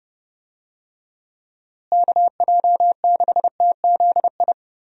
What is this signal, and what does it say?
Morse code, heard as a keyed tone, about 30 wpm, 700 Hz. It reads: KJ6TZI
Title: USA Call Signs - 6 Characters - Repeated 2x - 30wpm